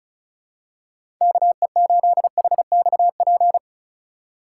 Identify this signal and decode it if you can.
Morse code, heard as a keyed tone, about 36 wpm, 700 Hz. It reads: KE8HXP
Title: USA Call Signs - 6 Characters - 2x Character Spacing - 35wpm